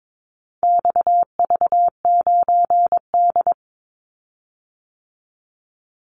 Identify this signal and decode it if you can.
Morse code, heard as a keyed tone, about 22 wpm, 700 Hz. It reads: XV9D
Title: Call Signs Repeated 3x - 22wpm